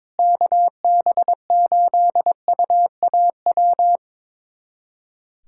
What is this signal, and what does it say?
Morse code, heard as a keyed tone, about 21 wpm, 700 Hz. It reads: KB8UAW
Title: USA Callsign DE Callsign 22wpm